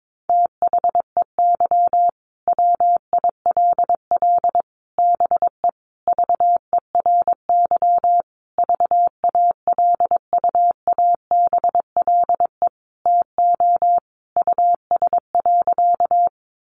Morse code, heard as a keyed tone, about 22 wpm, 700 Hz. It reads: THEY WILL BE VERY VALUABLE TO US.
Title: Sentences from Top 1100 Words 22wpm